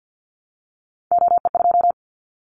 Morse code, heard as a keyed tone, about 50 wpm, 700 Hz. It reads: OE2
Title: Callsign Prefixes for Top 100 DXCC - 50wpm